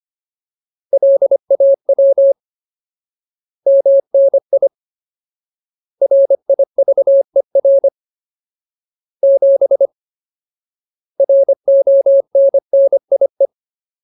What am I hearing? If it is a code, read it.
Morse code, heard as a keyed tone, about 25 wpm, 550 Hz. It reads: LAW MNI RIVER 7 RONNIE
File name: Ultimate Word List / 25WPM Ultimate